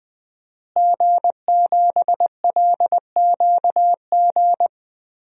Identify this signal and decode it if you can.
Morse code, heard as a keyed tone, about 20 wpm, 700 Hz. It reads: G7LQG
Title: DXCC-100 Call Signs - 5 Characters - Repeated 2x - 20wpm